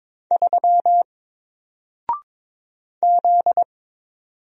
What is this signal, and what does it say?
Morse code, heard as a keyed tone, about 22 wpm, 700 Hz. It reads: 3 Z
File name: Single-Letter-Number-22wpm